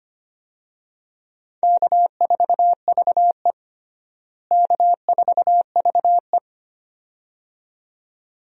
Morse code, heard as a keyed tone, about 25 wpm, 700 Hz. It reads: K4VE K4VE
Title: USA Call Signs - 4 Characters - Repeated 2x - 25wpm